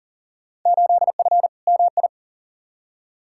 Morse code, heard as a keyed tone, about 40 wpm, 700 Hz. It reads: 8F MI
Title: Field Day Exchanges Encore - 40wpm